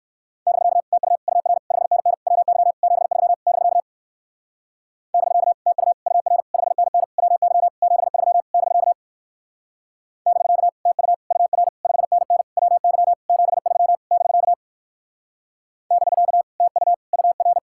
Morse code, heard as a keyed tone, about 59 wpm, 700 Hz. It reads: <BK> TU UR 5NN WY 73 <BK> <BK> TU UR 5NN WY 73 <BK> <BK> TU UR 5NN WY 73 <BK> <BK> TU UR